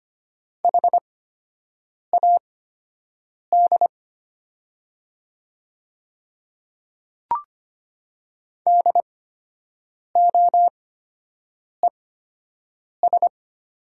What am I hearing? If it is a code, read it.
Morse code, heard as a keyed tone, about 25 wpm, spaced as for 7 wpm, 700 Hz. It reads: HAD DOES
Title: 8x Character Spacing - Top 200 Words - 25wpm